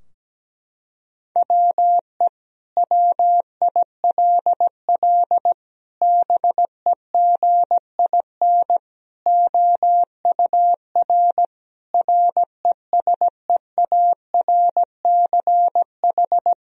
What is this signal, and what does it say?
Morse code, heard as a keyed tone, about 17 wpm, 700 Hz. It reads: WE WILL BEGIN OUR RESEARCH
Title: Sentences from Top 1000 Words - Review Part 4 - 17wpm